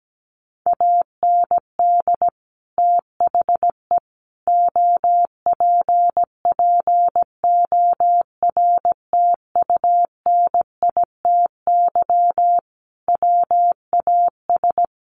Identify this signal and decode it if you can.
Morse code, heard as a keyed tone, about 17 wpm, 700 Hz. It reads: AND THE OPPORTUNITY WAS
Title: Sentences from Top 1400 Words 17wpm